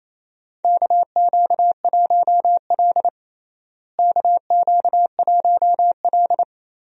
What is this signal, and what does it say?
Morse code, heard as a keyed tone, about 28 wpm, 700 Hz. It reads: KQ1L KQ1L
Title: USA Call Signs - 4 Characters - Repeated 3x - 28wpm